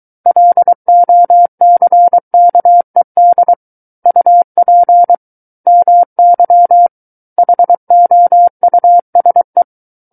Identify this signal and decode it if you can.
Morse code, heard as a keyed tone, about 23 wpm, 700 Hz. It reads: LOCKED UP MY HOUSE
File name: Book One / 14-Chapter 14 - 23 WPM